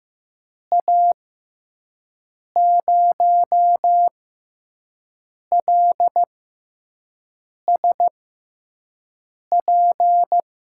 Morse code, heard as a keyed tone, about 15 wpm, spaced as for 5 wpm, 700 Hz. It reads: A0LSP
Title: USA Call Signs - 6x Character Spacing - 15wpm